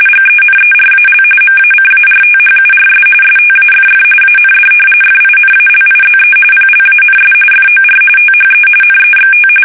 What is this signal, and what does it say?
RTTY, about 45.45 baud. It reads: GENERAL MACARTHURS STAFF AND FINALIZE PLANS FOR THE NEXT
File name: fsk850 / battleshipnj